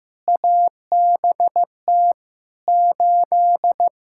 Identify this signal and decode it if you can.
Morse code, heard as a keyed tone, about 15 wpm, 700 Hz. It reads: ABT 8
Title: Practice QSO - Exchange 3 - More Info - 15wpm